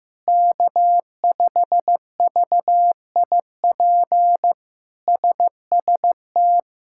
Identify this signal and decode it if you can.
Morse code, heard as a keyed tone, about 16 wpm, 700 Hz. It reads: K5VIP SST
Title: K1USN SST QSO Practice - 15wpm